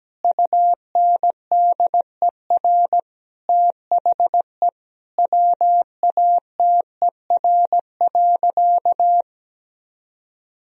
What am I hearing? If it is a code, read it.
Morse code, heard as a keyed tone, about 17 wpm, 700 Hz. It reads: UNDER THE WATER.
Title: Sentences from Top 1000 Words - Review Part 2 - 17wpm